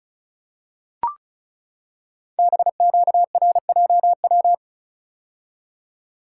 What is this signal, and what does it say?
Morse code, heard as a keyed tone, about 36 wpm, 700 Hz. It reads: BQRJW
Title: Code Groups - 5 Letters - 35wpm